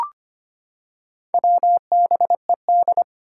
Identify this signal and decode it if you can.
Morse code, heard as a keyed tone, about 25 wpm, 700 Hz. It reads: WBED
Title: Four Challenging Letter-Number Combinations 25wpm